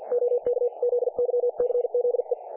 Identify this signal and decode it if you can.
Morse code, heard as a keyed tone, about 50 wpm, 500 Hz. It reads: OK ZYCZE
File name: dlazaawansowanych / nagraniazpasma / websdr recording 3564.7kHz